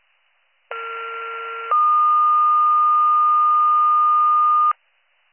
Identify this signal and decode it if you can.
Two-tone page, heard as a one-second tone, then a three-second tone, about 1370 then 1220 Hz. It is heard from 0.7 to 4.7 s.